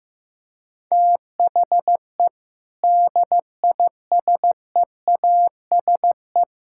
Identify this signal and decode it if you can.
Morse code, heard as a keyed tone, about 15 wpm, 700 Hz. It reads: THE DISEASE
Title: Sentences from Top 1000 Words - Review Part 6 - 15wpm